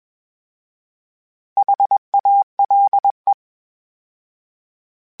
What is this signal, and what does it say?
Morse code, heard as a keyed tone, about 21 wpm, 800 Hz. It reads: HALE